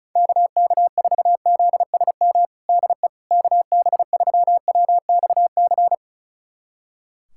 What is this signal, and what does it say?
Morse code, heard as a keyed tone, about 34 wpm, 700 Hz. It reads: KK4ZSM DE KB3WXC
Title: USA Callsign DE Callsign - 6 Characters - 35wpm